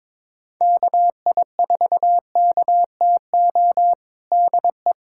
Morse code, heard as a keyed tone, about 22 wpm, 700 Hz. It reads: KI4KTO DE
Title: USA Callsign DE Callsign - 6 Characters - 22wpm